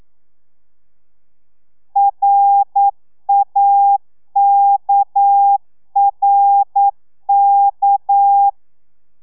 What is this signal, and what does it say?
Morse code, heard as a keyed tone, about 9 wpm, 800 Hz. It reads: RAKRK
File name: telegrafia / i2tfj / Lez6 40